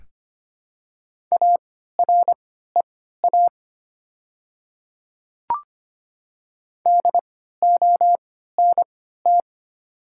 Morse code, heard as a keyed tone, about 25 wpm, spaced as for 14 wpm, 690 Hz. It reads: AREA DONT
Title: 3x Character Spacing - Top 500 Words - 25wpm